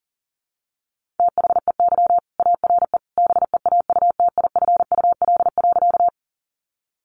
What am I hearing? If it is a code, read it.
Morse code, heard as a keyed tone, about 40 wpm, 700 Hz. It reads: THEY ARE BEAUTIFUL.
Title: Sentences from Top 900 Words 40wpm